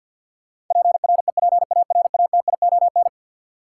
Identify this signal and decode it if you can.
Morse code, heard as a keyed tone, about 53 wpm, 700 Hz. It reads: PREPARATION
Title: Sentences from Top 1400 Words - New Word Drill - 50wpm